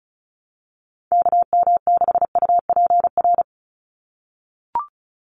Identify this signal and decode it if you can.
Morse code, heard as a keyed tone, about 34 wpm, 700 Hz. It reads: KM6UPR
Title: USA Call Signs 35wpm